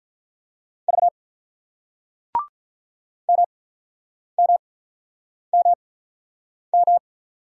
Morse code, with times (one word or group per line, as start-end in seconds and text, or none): 0.9-1.1 V
3.3-7.0 MMMM